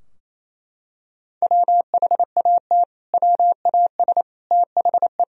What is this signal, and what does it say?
Morse code, heard as a keyed tone, about 28 wpm, 700 Hz. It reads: WHAT WAS THE
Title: Sentences from Top 1000 Words - Review Part 3 - 28wpm